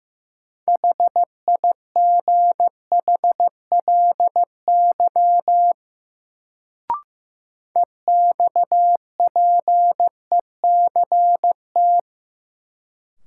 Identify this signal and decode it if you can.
Morse code, heard as a keyed tone, about 15 wpm, 700 Hz. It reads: HIGHLY EXPECT
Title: Sentences from Top 1100 Words - New Word Drill - 15wpm